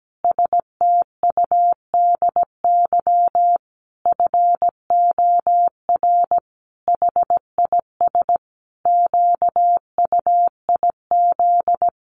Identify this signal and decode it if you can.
Morse code, heard as a keyed tone, about 17 wpm, 700 Hz. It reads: STUDY FOR HIS QUIZ
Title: Additional Easy Phrases 17wpm